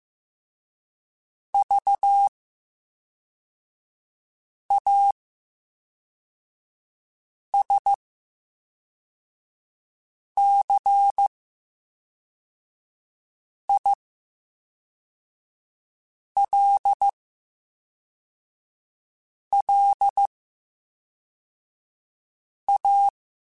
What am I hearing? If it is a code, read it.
Morse code, heard as a keyed tone, about 15 wpm, 800 Hz. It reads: V A S C I L L A
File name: Special INGs / Special INGs-mp3s / VerySlowToFaster Ing @10